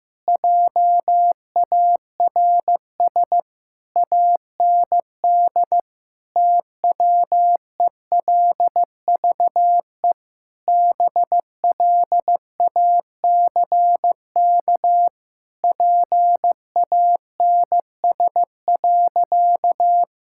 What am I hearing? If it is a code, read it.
Morse code, heard as a keyed tone, about 15 wpm, 700 Hz. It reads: JARS AND TWELVE BLACK PANS.
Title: Pangram Challenge v2 - 15wpm